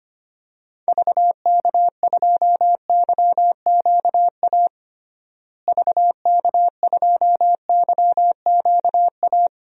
Morse code, heard as a keyed tone, about 25 wpm, 700 Hz. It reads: VK2YQA VK2YQA
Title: DXCC-100 Call Signs - 6 Characters - Repeated 2x - 25wpm